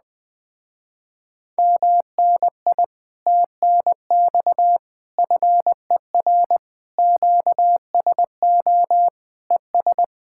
Morse code, heard as a keyed tone, about 20 wpm, 700 Hz. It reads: MNI TNX FER QSO ES
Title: Speed-Racing Common QSO Phrases - 17wpm